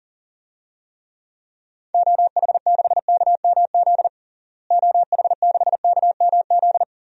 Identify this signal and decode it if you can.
Morse code, heard as a keyed tone, about 40 wpm, 700 Hz. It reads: OH6KMZ OH6KMZ
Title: DXCC-100 Call Signs - 6 Characters - Repeated 3x - 40wpm